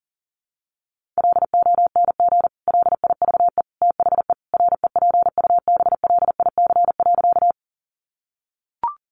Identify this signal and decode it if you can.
Morse code, heard as a keyed tone, about 40 wpm, 700 Hz. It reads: LONG LIVE THE REPUBLIC.
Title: Sentences from Top 1500 Words 40wpm